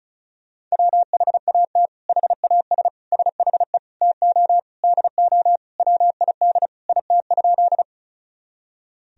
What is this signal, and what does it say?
Morse code, heard as a keyed tone, about 34 wpm, 700 Hz. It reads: WHAT HAS SHE TO DO WID IT?
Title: Sentences from Top 100 Words and Abbreviations 35wpm